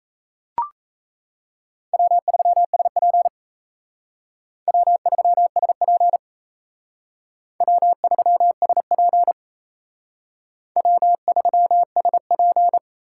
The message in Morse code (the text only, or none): W3SP W3SP W3SP W3SP